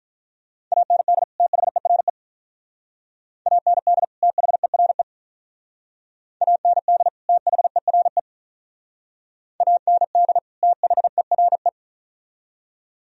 Morse code, heard as a keyed tone, about 45 wpm, 700 Hz. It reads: AND THERE AND THERE AND THERE AND THERE